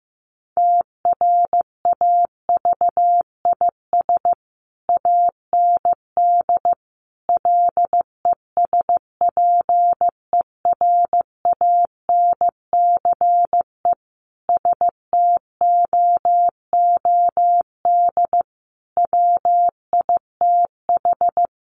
Morse code, heard as a keyed tone, about 14 wpm, 700 Hz. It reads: TRAVIS AND LESPERANCE STOOD WITH
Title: A Sound Of Thunder 15wpm